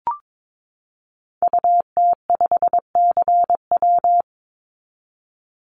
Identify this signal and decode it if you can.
Morse code, heard as a keyed tone, about 22 wpm, 700 Hz. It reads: UT5CW
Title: Call Signs Repeated 2x - 22wpm